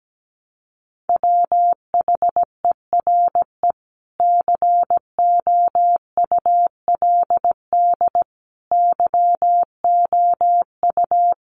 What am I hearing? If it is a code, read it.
Morse code, heard as a keyed tone, about 17 wpm, 700 Hz. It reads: WHERE COULD YOU
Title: Sentences from Top 300 Words 17wpm